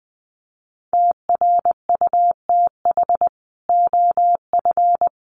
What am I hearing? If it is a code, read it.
Morse code, heard as a keyed tone, about 20 wpm, 700 Hz. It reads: TRUTH OF
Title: Sets of 5 Words Encore2 20wpm